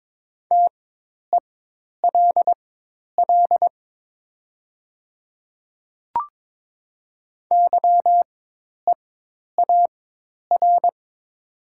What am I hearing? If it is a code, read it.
Morse code, heard as a keyed tone, about 22 wpm, spaced as for 10 wpm, 700 Hz. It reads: TELL YEAR
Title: 4x Character Spacing - Top 300 Words - 22wpm